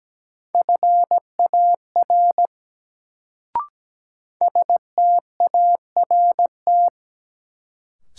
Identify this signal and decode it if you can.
Morse code, heard as a keyed tone, about 17 wpm, 700 Hz. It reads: FAR START
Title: Sentences from Top 300 Words - New Word Drill - 17wpm